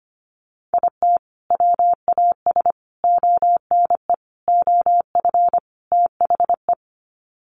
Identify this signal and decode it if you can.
Morse code, heard as a keyed tone, about 25 wpm, 700 Hz. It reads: IT WAS ONE OF THE
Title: Sets of 5 Words 25wpm